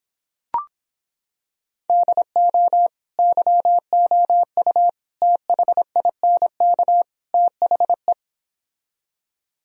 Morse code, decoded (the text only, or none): DO YOU THINK THE